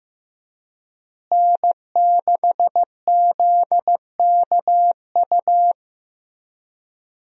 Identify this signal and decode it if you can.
Morse code, heard as a keyed tone, about 15 wpm, 700 Hz. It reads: N6ZKU
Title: USA Call Signs - 5 Characters - Repeated 2x - 15wpm